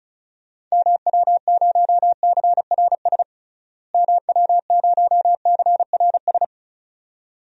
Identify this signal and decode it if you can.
Morse code, heard as a keyed tone, about 36 wpm, 700 Hz. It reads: MW0CRS MW0CRS
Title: DXCC-100 Call Signs - 6 Characters - Repeated 2x - 35wpm